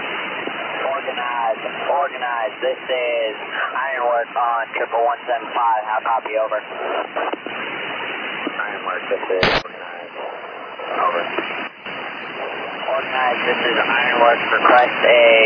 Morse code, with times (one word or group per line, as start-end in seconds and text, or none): none